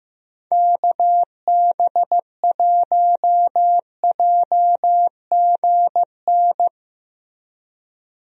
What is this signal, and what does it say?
Morse code, heard as a keyed tone, about 15 wpm, 700 Hz. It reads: KB1JGN
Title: USA Call Signs - 6 Characters - Repeated 3x - 15wpm